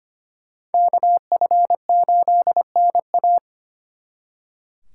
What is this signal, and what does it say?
Morse code, heard as a keyed tone, about 32 wpm, 700 Hz. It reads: KF8NA